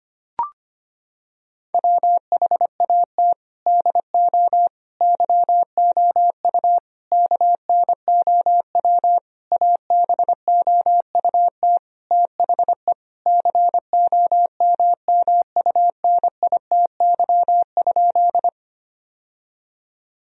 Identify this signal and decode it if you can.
Morse code, heard as a keyed tone, about 25 wpm, 700 Hz. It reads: WHAT DO YOU KNOW ABOUT THE COMMUNITY?
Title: Sentences from Top 1000 Words - Review Part 4 - 25wpm